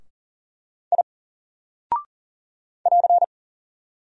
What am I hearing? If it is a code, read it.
Morse code, heard as a keyed tone, about 42 wpm, 700 Hz. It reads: I <AR>
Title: Single Character (Letter:Number:Punctuation:Prosign) 40wpm